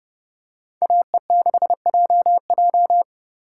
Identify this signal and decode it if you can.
Morse code, heard as a keyed tone, about 30 wpm, 700 Hz. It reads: AE6JJ